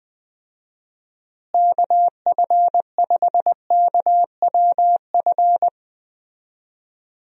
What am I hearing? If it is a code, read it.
Morse code, heard as a keyed tone, about 20 wpm, 700 Hz. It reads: KF5KWF
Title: USA Call Signs - 6 Characters - Repeated 2x - 20wpm